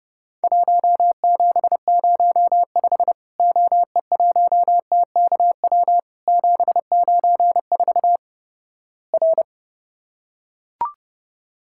Morse code, heard as a keyed tone, about 31 wpm, 700 Hz. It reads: 1705 OE1TKW 794 R
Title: Worked All Europe DX Contest - QTC Exchange - 30wpm